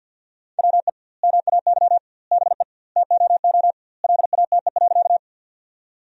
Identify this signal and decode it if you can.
Morse code, heard as a keyed tone, about 50 wpm, 700 Hz. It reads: WE MAY BE TOO LATE.
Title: 2x Word-Spacing - Sentences from Top 300 Words - 50wpm